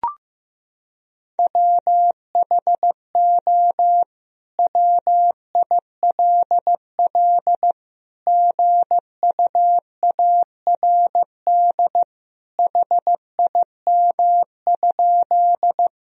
Morse code, heard as a keyed tone, about 15 wpm, 700 Hz. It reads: WHO WILL GUARD HIM?
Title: Sentences from Top 1000 Words - Review Part 4 - 15wpm